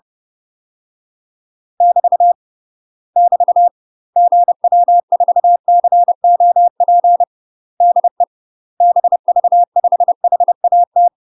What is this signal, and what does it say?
Morse code, heard as a keyed tone, about 30 wpm, 700 Hz. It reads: <BT> <BT> GW4COP DE BV5HAT
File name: UK MASTER QSOs TRAINING / 30WPM / UK MASTER QSOs 0008